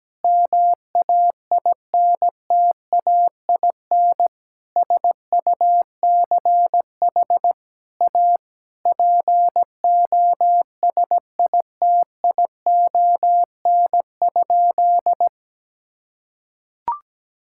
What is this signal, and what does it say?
Morse code, heard as a keyed tone, about 17 wpm, 700 Hz. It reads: MAINTAIN SUCH A POSITION?
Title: Sentences from Top 1000 Words - Review Part 5 - 17wpm